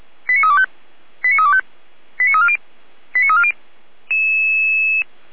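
Five-tone selective call (ZVEI), heard as quick runs of five tones, tones of about 70 ms. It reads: 89325 89325 89340 89340
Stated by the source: ZVEI five-tone selective call on 455 received by SDR.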